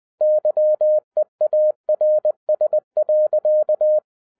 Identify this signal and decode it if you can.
Morse code, heard as a keyed tone, about 20 wpm, 610 Hz. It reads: YEARS.